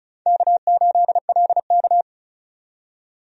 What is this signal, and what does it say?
Morse code, heard as a keyed tone, about 34 wpm, 700 Hz. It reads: K8LK
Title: USA Call Signs - 4 Characters - 35wpm